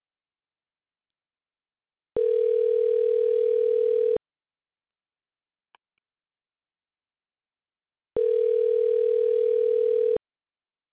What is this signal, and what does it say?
Ringback tone (North America), 440 + 480 Hz.